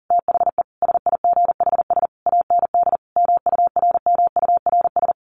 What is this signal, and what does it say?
Morse code, heard as a keyed tone, about 40 wpm, 700 Hz. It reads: THE SIGHS AND MURMURS
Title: A Sound Of Thunder 40wpm